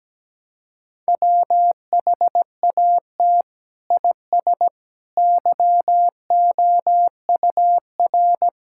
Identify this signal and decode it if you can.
Morse code, heard as a keyed tone, about 17 wpm, 700 Hz. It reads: WHAT IS YOUR
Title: Sentences from Top 1000 Words - Review Part 6 - 17wpm